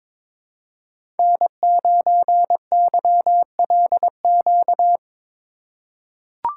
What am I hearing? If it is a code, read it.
Morse code, heard as a keyed tone, about 22 wpm, 700 Hz. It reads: N9YLQ
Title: USA Call Signs - 5 Characters - 8x Character Spacing - 22wpm